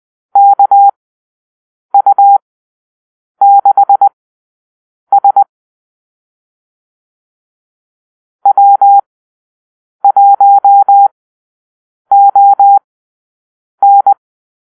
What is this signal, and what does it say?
Morse code, heard as a keyed tone, about 20 wpm, spaced as for 7 wpm, 800 Hz. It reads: KU6S W1ON
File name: arrl / Call-Sign Drill-8